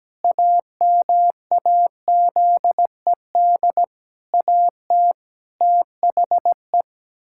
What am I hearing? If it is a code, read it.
Morse code, heard as a keyed tone, about 17 wpm, 700 Hz. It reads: AMAZED AT THE
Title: Pangram Challenge v2 - 17wpm